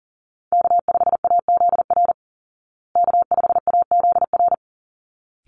Morse code, heard as a keyed tone, about 40 wpm, 700 Hz. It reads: K5AZR K5AZR
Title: Call Signs Repeated 3x - 40wpm